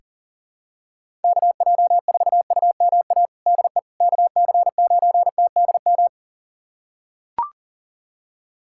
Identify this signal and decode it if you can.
Morse code, heard as a keyed tone, about 40 wpm, 700 Hz. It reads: KJ4UMA DE KC9TDM
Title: USA Callsign DE Callsign - 6 Characters - 40wpm